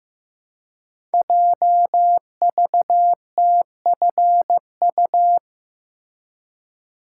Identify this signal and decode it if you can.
Morse code, heard as a keyed tone, about 15 wpm, 700 Hz. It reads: JVTFU
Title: Code Groups - 5 Letter:Numbers - 15wpm